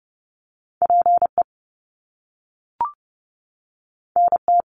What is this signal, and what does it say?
Morse code, heard as a keyed tone, about 30 wpm, 700 Hz. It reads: PE NT
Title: Canadian Provinces and Territories 30wpm